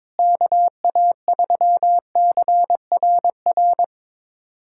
Morse code, heard as a keyed tone, about 22 wpm, 700 Hz. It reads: KA3CRR
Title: USA Callsign DE Callsign - 6 Characters - 22wpm